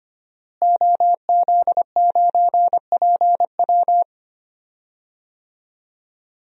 Morse code, heard as a keyed tone, about 25 wpm, 700 Hz. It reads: OZ9PW
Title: DXCC-100 Call Signs - 5 Characters - Repeated 2x - 25wpm